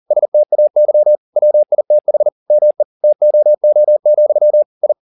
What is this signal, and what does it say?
Morse code, heard as a keyed tone, about 40 wpm, 600 Hz. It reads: STAY WITH ME TOO, I